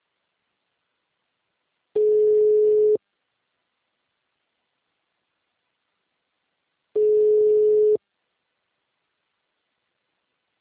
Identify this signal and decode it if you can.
Ringback tone (Europe, 425 Hz).